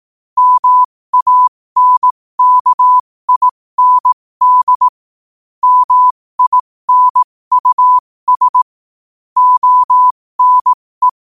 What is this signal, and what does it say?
Morse code, heard as a keyed tone, about 18 wpm, spaced as for 15 wpm, 1000 Hz. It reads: MANKIND MINUS ONE
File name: qotdfiles / 15wpm